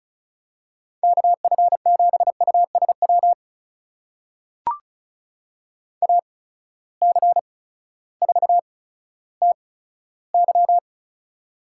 Morse code, heard as a keyed tone, about 36 wpm, spaced as for 16 wpm, 700 Hz. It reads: KF7USW AC4TY
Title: USA Call Signs - 8x Character Spacing - 35wpm